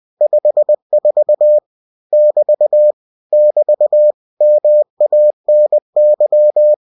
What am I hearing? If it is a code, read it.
Morse code, heard as a keyed tone, about 20 wpm, 600 Hz. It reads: 54 <BT> <BT> MANY
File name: cw-books / winnie-the-pooh / winnie-20wpm-page-0054